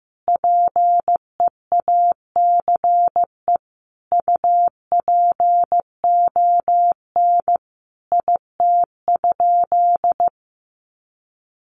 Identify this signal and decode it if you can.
Morse code, heard as a keyed tone, about 15 wpm, 700 Hz. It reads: PEACE UPON IT?
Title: Sentences from Top 1500 Words 15wpm